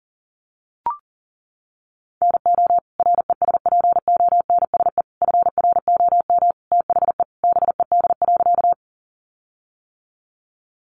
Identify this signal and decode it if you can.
Morse code, heard as a keyed tone, about 40 wpm, 700 Hz. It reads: NO RESPONSE FROM THE BED.
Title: Sentences from Top 1400 Words 40wpm